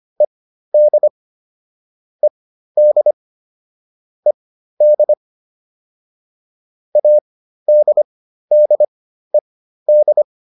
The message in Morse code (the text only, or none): ED ED ED ADDED